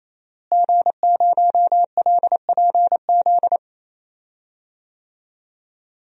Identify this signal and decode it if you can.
Morse code, heard as a keyed tone, about 28 wpm, 700 Hz. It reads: G0LPZ
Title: DXCC-100 Call Signs - 5 Characters - Repeated 3x - 28wpm